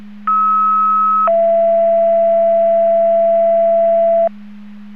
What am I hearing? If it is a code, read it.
Two-tone page, heard as a one-second tone, then a three-second tone, about 1290 then 670 Hz. It is heard from 0.3 to 4.3 s.